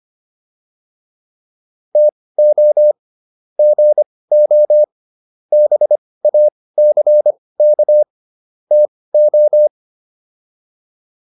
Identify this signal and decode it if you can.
Morse code, heard as a keyed tone, about 25 wpm, spaced as for 18 wpm, 600 Hz. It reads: TO GO BACK TO